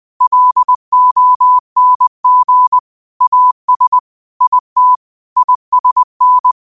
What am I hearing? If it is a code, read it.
Morse code, heard as a keyed tone, about 20 wpm, 1000 Hz. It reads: LONG AS IT ISN